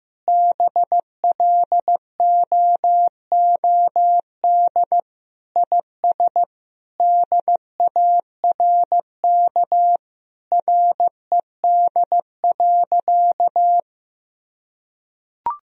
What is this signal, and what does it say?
Morse code, heard as a keyed tone, about 15 wpm, 700 Hz. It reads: BLOOD IS DARK RED.
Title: Sentences from Top 1000 Words - Review Part 2 - 15wpm